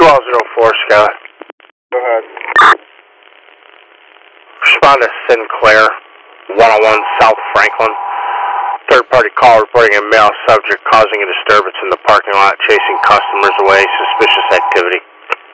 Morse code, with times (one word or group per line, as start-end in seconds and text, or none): none